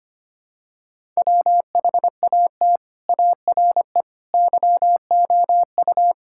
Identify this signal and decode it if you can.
Morse code, heard as a keyed tone, about 25 wpm, 700 Hz. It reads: WHAT ARE YOU
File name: practice / 25wpm / Sentences with 4-Word Phrases - 25wpm